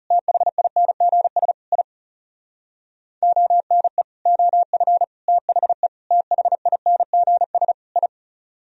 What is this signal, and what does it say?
Morse code, heard as a keyed tone, about 37 wpm, 700 Hz. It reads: THINGS I ONE OF THE THINGS I
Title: Speed-Racing - Sets of 5 Words 30wpm